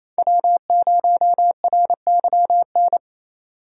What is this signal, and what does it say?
Morse code, heard as a keyed tone, about 28 wpm, 700 Hz. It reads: W0RYN
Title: USA Call Signs - 5 Characters - 6x Character Spacing - 28wpm